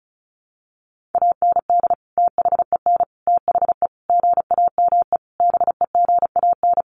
Morse code, heard as a keyed tone, about 34 wpm, 700 Hz. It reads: AND THEN THE GAME BEGAN
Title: Sentences from Top 500 Words 35wpm